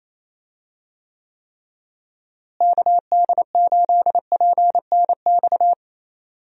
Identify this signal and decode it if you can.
Morse code, heard as a keyed tone, about 28 wpm, 700 Hz. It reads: KD8PNX
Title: USA Call Signs - 6 Characters - Repeated 3x - 28wpm